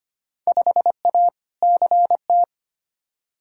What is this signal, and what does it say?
Morse code, heard as a keyed tone, about 25 wpm, 700 Hz. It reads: 5A CT
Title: Field Day Exchanges - 25wpm